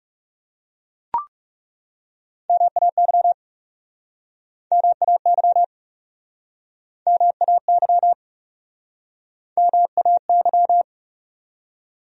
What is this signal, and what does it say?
Morse code, heard as a keyed tone, about 40 wpm, 700 Hz. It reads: MAY MAY MAY MAY